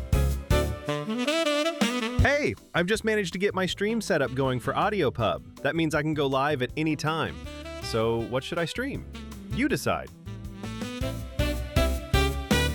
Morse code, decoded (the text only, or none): none